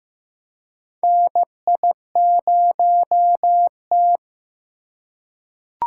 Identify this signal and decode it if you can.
Morse code, heard as a keyed tone, about 15 wpm, 700 Hz. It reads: NI0T
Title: USA Call Signs - 4 Characters - 3x Character Spacing - 15wpm